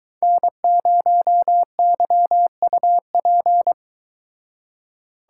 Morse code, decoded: N0YUP

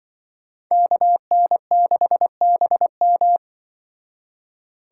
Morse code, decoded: KN6BM

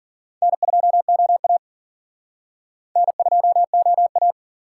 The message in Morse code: N2OA N2OA

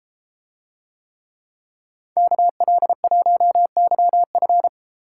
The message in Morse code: KL1YF

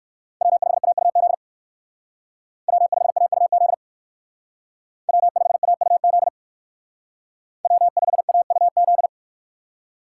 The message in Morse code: W5AUZ W5AUZ W5AUZ W5AUZ

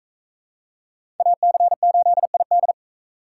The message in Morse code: AC8ID

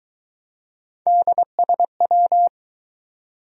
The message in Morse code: DSW

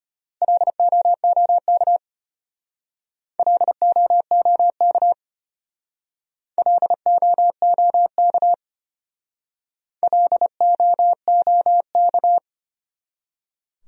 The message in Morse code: LOOK LOOK LOOK LOOK

